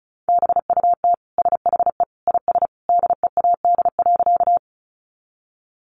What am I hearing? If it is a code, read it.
Morse code, heard as a keyed tone, about 34 wpm, 700 Hz. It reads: BUT SHE IS DEAD.
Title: Sentences from Top 1500 Words 35wpm